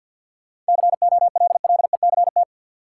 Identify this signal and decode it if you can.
Morse code, heard as a keyed tone, about 53 wpm, 700 Hz. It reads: COLLECT
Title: Sentences from Top 1600 Words - New Word Drill - 50wpm